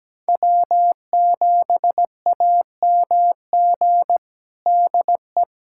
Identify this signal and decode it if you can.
Morse code, heard as a keyed tone, about 17 wpm, 700 Hz. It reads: W7AMG DE
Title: USA Callsign DE Callsign - 5 Characters - 17wpm